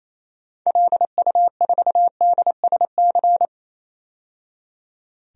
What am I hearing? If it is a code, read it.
Morse code, heard as a keyed tone, about 28 wpm, 700 Hz. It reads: LU4DSC
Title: DXCC-100 Call Signs - 6 Characters - 28wpm